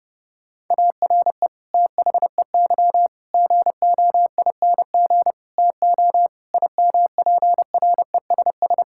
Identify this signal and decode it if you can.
Morse code, heard as a keyed tone, about 30 wpm, 700 Hz. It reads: ARE THEY GOING TO IMPRESS
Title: Sentences from Top 1600 Words 30wpm